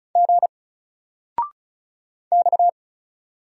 Morse code, heard as a keyed tone, about 35 wpm, 700 Hz. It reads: G X
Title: Single Letter-Number Alt 35wpm